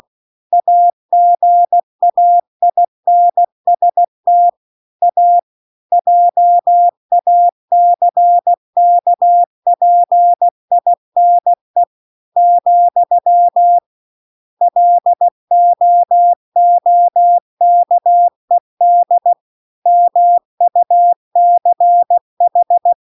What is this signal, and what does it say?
Morse code, heard as a keyed tone, about 16 wpm, 700 Hz. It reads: AGAINST A JACKPINE , LOOKED MUCH